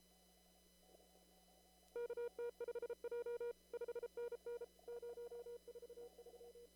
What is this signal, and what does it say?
Morse code, heard as a keyed tone, about 34 wpm, 480 Hz. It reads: KT5J 5NN 043
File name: CW / kt5j wpx cw-op-k5pi